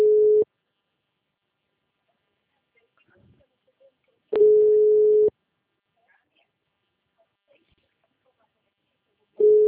Ringback tone (Europe, 425 Hz).
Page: Rio 3 - Ura